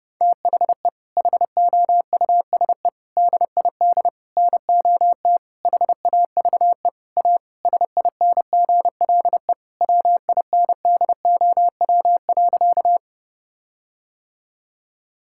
Morse code, heard as a keyed tone, about 32 wpm, 700 Hz. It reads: THE HOUSE DID NOT HAVE A SINGLE WINDOW.
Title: Sentences from Top 1600 Words 30wpm